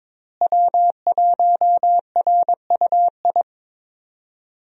Morse code, heard as a keyed tone, about 23 wpm, 700 Hz. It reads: W1RUI
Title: USA Call Signs - 5 Characters - Repeated 2x - 22wpm